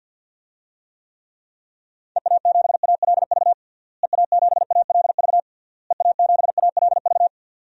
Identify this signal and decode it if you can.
Morse code, heard as a keyed tone, about 48 wpm, 700 Hz. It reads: EA7ALV EA7ALV EA7ALV
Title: DXCC-100 Call Signs - 6 Characters - Repeated 3x - 50wpm